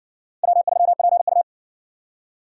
Morse code, heard as a keyed tone, about 58 wpm, 700 Hz. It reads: W3PU